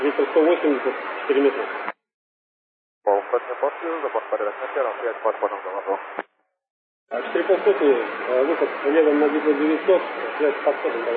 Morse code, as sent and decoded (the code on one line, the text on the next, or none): none